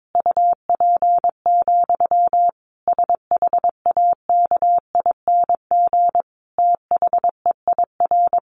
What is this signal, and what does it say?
Morse code, heard as a keyed tone, about 22 wpm, 700 Hz. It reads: UP, SHAKING THEIR
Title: A Sound Of Thunder 22wpm